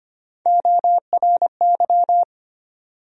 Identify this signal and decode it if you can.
Morse code, heard as a keyed tone, about 25 wpm, 700 Hz. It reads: ORY
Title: USA Call Sign Postfixes - 3 Letters 25wpm